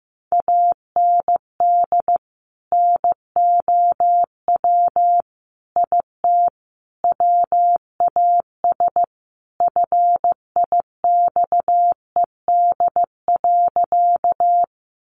Morse code, heard as a keyed tone, about 15 wpm, 700 Hz. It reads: AND NOW IT WAS FIXED.
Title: Sentences from Top 1100 Words 15wpm